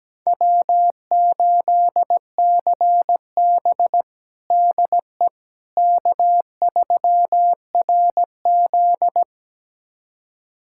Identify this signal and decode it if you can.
Morse code, heard as a keyed tone, about 17 wpm, 700 Hz. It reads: W8CB DE K3RZ
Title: USA Callsign DE Callsign - 4 Characters - 17wpm